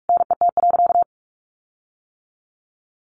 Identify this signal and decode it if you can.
Morse code, heard as a keyed tone, about 44 wpm, 700 Hz. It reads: NET.
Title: Sentences from Top 1500 Words 45wpm